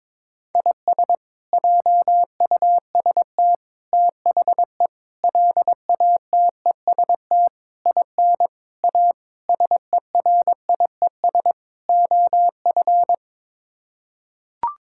Morse code, heard as a keyed tone, about 22 wpm, 700 Hz. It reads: IS JUST THE LATEST IN A SERIES OF
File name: practice / 22wpm / Sets of 8 Words - 22wpm